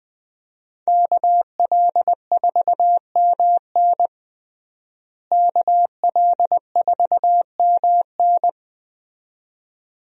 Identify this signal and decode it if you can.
Morse code, heard as a keyed tone, about 20 wpm, 700 Hz. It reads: KL4MN KL4MN
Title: USA Call Signs - 5 Characters - Repeated 2x - 20wpm